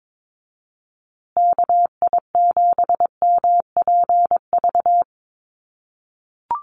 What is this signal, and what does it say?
Morse code, heard as a keyed tone, about 22 wpm, 700 Hz. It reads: KI7MPV
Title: USA Call Signs Repeated 3x 22wpm